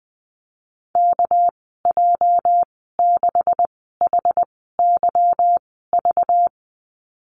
Morse code, heard as a keyed tone, about 19 wpm, spaced as for 14 wpm, 700 Hz. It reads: KJ6HYV
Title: 231 Lesson 37g - Letter x - Call Signs 20wpm